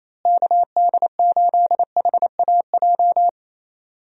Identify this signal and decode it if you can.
Morse code, heard as a keyed tone, about 28 wpm, 700 Hz. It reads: KD8HAJ
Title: USA Call Signs - 6 Characters - Repeated 3x - 28wpm